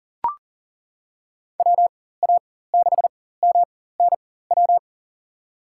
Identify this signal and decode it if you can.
Morse code, heard as a keyed tone, about 38 wpm, spaced as for 18 wpm, 700 Hz. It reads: WA6MNW
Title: USA Call Signs - 6 Characters - 4x Character Spacing - 40wpm